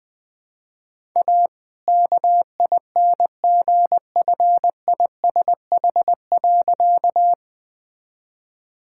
Morse code, heard as a keyed tone, about 20 wpm, 700 Hz. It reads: A KINGFISH.
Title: All-Age Appropriate Jokes - 20wpm